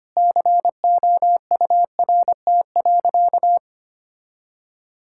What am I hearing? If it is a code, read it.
Morse code, heard as a keyed tone, about 25 wpm, 700 Hz. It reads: COURT.
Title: Sentences from Top 1600 Words 25wpm